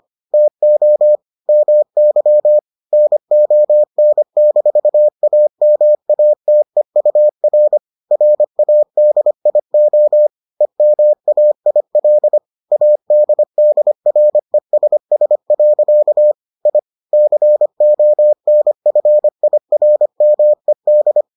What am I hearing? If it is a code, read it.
Morse code, heard as a keyed tone, about 25 wpm, 600 Hz. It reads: TO MY NON-AMATEUR RADIO EMAIL ADDRESS. I CONFIRMED